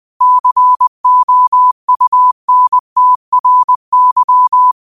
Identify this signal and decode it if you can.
Morse code, heard as a keyed tone, about 20 wpm, 1000 Hz. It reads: COUNTRY
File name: qotdfiles / 20wpm